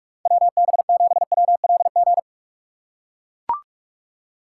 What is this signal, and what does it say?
Morse code, heard as a keyed tone, about 42 wpm, 700 Hz. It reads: WB7WLG